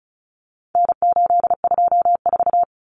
Morse code, heard as a keyed tone, about 34 wpm, 700 Hz. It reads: N824
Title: Common Cut Numbers - 4-Digit 35wpm